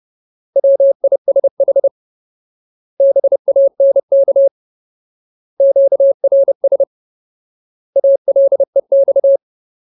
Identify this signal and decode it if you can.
Morse code, heard as a keyed tone, about 30 wpm, 550 Hz. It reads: WISH BANK QRS ALEX
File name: Ultimate Word List / 30WPM Ultimate